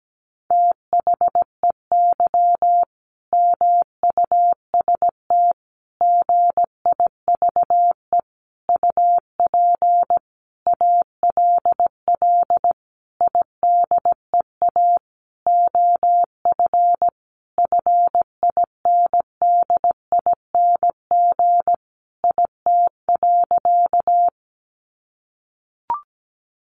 Morse code, heard as a keyed tone, about 17 wpm, 700 Hz. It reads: THEY MUST GIVE UP ALL IDEA OF FINDING IT.
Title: Sentences from Top 1100 Words 17wpm